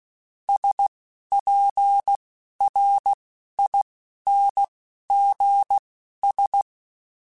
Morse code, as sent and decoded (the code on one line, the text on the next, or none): ... .--. .-. .. -. --. ...
SPRINGS